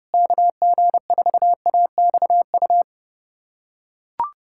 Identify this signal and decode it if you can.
Morse code, heard as a keyed tone, about 30 wpm, 700 Hz. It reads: KG4AXU
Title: USA Callsign DE Callsign - 6 Characters - 30wpm